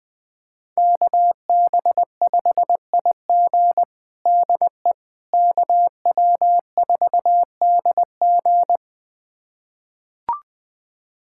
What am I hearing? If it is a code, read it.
Morse code, heard as a keyed tone, about 21 wpm, 700 Hz. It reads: KB5IG DE KW4DG
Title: USA Callsign DE Callsign - 5 Characters - 20wpm